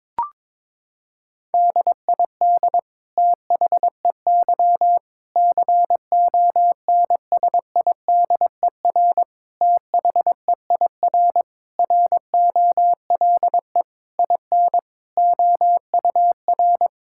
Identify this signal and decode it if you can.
Morse code, heard as a keyed tone, about 22 wpm, 700 Hz. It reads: DID THEY CONSIDER THEIR ROLE IN OUR
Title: Sentences from Top 1000 Words - Review Part 6 - 22wpm